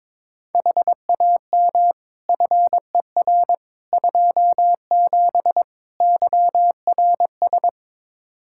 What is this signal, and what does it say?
Morse code, heard as a keyed tone, about 22 wpm, 700 Hz. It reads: HAM FER 27 YRS
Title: Practice QSO - Exchange 3 - More Info - 22wpm